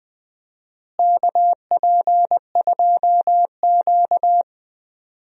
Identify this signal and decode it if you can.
Morse code, heard as a keyed tone, about 20 wpm, 700 Hz. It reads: KP2Q
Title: DXCC-100 Call Signs - 4 Characters - Repeated 2x - 20wpm